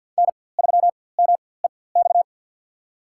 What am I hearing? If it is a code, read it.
Morse code, heard as a keyed tone, about 47 wpm, spaced as for 23 wpm, 700 Hz. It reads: N3MEX
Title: USA Call Signs - 4x Character Spacing - 50wpm